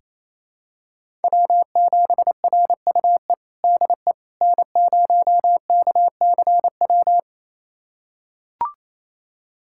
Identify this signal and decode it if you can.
Morse code, heard as a keyed tone, about 28 wpm, 700 Hz. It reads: W7RUE DE N0KCW
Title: USA Callsign DE Callsign - 5 Characters - 28wpm